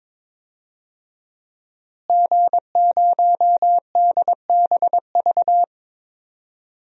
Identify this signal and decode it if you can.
Morse code, heard as a keyed tone, about 22 wpm, 700 Hz. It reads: G0DBV
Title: DXCC-100 Call Signs - 5 Characters - Repeated 2x - 22wpm